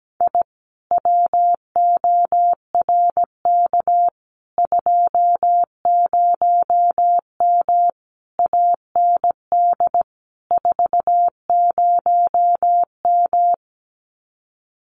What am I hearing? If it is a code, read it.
Morse code, heard as a keyed tone, about 17 wpm, 700 Hz. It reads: I WORK 20M AND 40M
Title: Common QSO Phrases Encore 17wpm